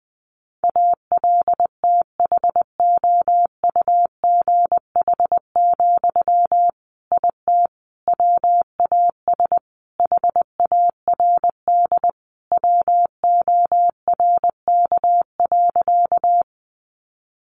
Morse code, heard as a keyed tone, about 20 wpm, 700 Hz. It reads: ALTHOUGH, IT WAS HARD WORK.
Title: Sentences from Top 600 Words 20wpm